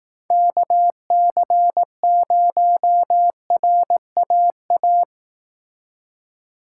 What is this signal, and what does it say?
Morse code, heard as a keyed tone, about 19 wpm, 700 Hz. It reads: KC0RAA